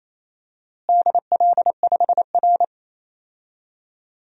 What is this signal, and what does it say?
Morse code, heard as a keyed tone, about 27 wpm, 700 Hz. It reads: DL5R
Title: DXCC-100 Call Signs - 4 Characters - Repeated 3x - 28wpm